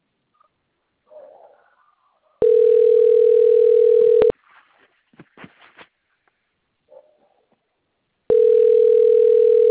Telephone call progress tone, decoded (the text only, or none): ringback tone